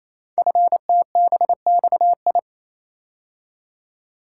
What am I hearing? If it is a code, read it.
Morse code, heard as a keyed tone, about 28 wpm, 700 Hz. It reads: FTBXI
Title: Code Groups - 5 Letters - 28wpm